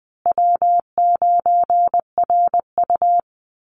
Morse code, heard as a keyed tone, about 20 wpm, 700 Hz. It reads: W9RU
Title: USA Call Signs Repeated 2x 20wpm